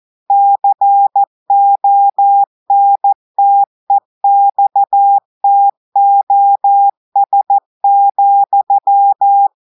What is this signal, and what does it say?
Morse code, heard as a keyed tone, about 14 wpm, 800 Hz. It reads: CONTEXTOS,
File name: audios / portugais / portugais texte en clair 1 vitesse 14 mots minutes